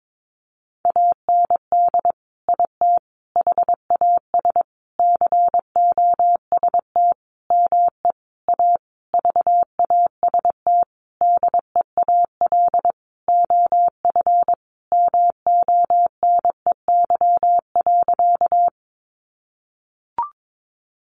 Morse code, heard as a keyed tone, about 22 wpm, 700 Hz. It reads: AND IT HAS COST ME A VAST DEAL OF MONEY.
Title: Sentences from Top 1300 Words 22wpm